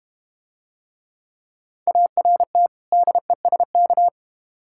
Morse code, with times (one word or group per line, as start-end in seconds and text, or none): 1.9-2.7 ART
2.9-4.1 DESK